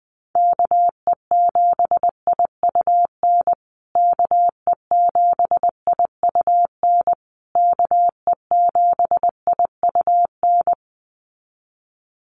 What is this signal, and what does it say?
Morse code, heard as a keyed tone, about 20 wpm, 700 Hz. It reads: KE7IUN KE7IUN KE7IUN
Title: USA Call Signs Repeated 3x 20wpm